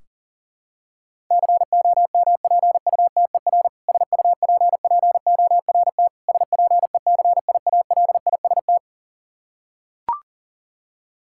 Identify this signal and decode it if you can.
Morse code, heard as a keyed tone, about 40 wpm, 700 Hz. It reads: COMPUTER SUPPORT SPECIALIST
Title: K1USN SST QSO Practice 40wpm